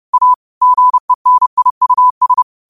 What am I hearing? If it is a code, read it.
Morse code, heard as a keyed tone, about 30 wpm, 1000 Hz. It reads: A GENIUS
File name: qotdfiles / 30wpm